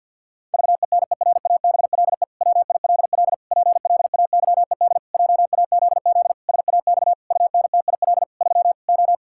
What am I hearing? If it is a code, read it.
Morse code, heard as a keyed tone, about 50 wpm, 700 Hz. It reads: VENERABLE WILL PLAYED JAZZ SAX UNTIL 3 O
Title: Pangram Challenge v2 - 50wpm